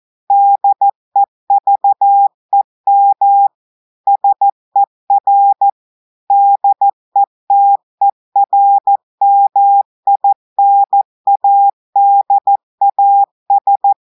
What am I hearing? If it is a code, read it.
Morse code, heard as a keyed tone, about 14 wpm, 800 Hz. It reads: DEVEM SER DETERMINADAS
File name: audios / portugais / portugais texte en clair 2 vitesse 14 mots minutes